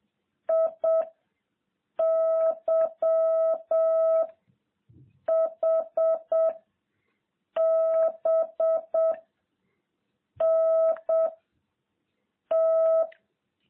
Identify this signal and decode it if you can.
Morse code, heard as a keyed tone, about 7 wpm, 660 Hz. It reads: I Y H B N T